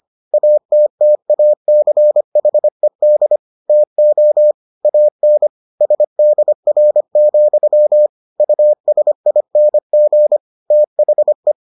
Morse code, heard as a keyed tone, about 25 wpm, 600 Hz. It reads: ATTACHED TO AN SDR, USING THE